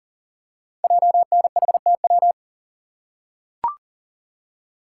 Morse code, heard as a keyed tone, about 40 wpm, 690 Hz. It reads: JNHTW
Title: Code Groups - 5 Letters - 40wpm